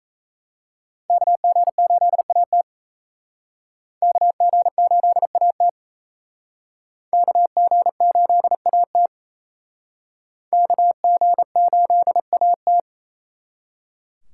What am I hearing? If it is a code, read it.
Morse code, heard as a keyed tone, about 35 wpm, 700 Hz. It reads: KG8AT KG8AT KG8AT KG8AT
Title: Speed Racing - USA 2x2 Call Signs 28wpm